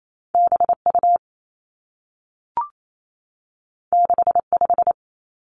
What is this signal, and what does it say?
Morse code, heard as a keyed tone, about 28 wpm, 700 Hz. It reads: B U 6 5
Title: Two Challenging Letter-Number Combinations 28wpm